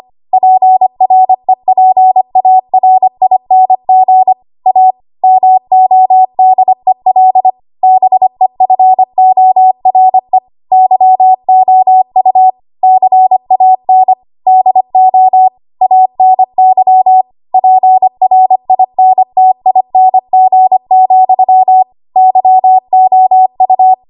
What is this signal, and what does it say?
Morse code, heard as a keyed tone, about 25 wpm, 750 Hz. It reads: PREPARING A MODEL BEFORE YOU CAN DO ANY PRINTING, YOU
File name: Archive / 25 WPM / 25WPM